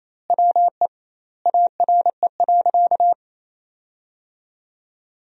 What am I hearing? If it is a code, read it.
Morse code, heard as a keyed tone, about 28 wpm, 700 Hz. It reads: WE ARE.
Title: 2x Word-Spacing - Sentences from Top 200 Words - 28wpm